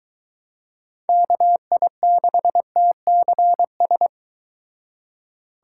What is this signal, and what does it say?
Morse code, heard as a keyed tone, about 23 wpm, 700 Hz. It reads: KI6TCS